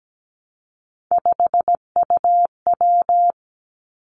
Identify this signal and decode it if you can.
Morse code, heard as a keyed tone, about 17 wpm, 700 Hz. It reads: 5UW
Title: Three Challenging Letter-Number Combinations 17wpm